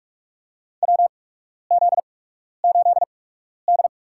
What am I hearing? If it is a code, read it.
Morse code, heard as a keyed tone, about 42 wpm, spaced as for 12 wpm, 700 Hz. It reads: WZ8D
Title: USA Call Signs - 8x Character Spacing - 45wpm